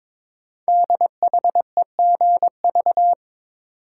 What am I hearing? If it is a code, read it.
Morse code, heard as a keyed tone, about 22 wpm, 700 Hz. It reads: DHEGV
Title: Code Groups - 5 Letters - 22wpm